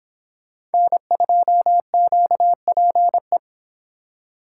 Morse code, heard as a keyed tone, about 28 wpm, 700 Hz. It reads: N2QPE